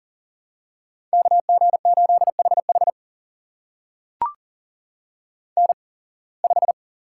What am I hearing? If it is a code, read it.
Morse code, heard as a keyed tone, about 40 wpm, spaced as for 26 wpm, 700 Hz. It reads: KG8HH N5
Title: USA Call Signs - 5 Characters - 8x Character Spacing - 40wpm